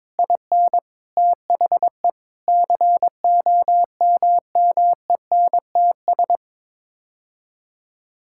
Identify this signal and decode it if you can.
Morse code, heard as a keyed tone, about 22 wpm, 700 Hz. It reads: IN THE COMMENTS
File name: Sets of 9 Words - 22wpm